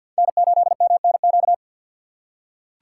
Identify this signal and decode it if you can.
Morse code, heard as a keyed tone, about 47 wpm, 700 Hz. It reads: N8MNQ
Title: USA Call Signs - 5 Characters - 6x Character Spacing - 50wpm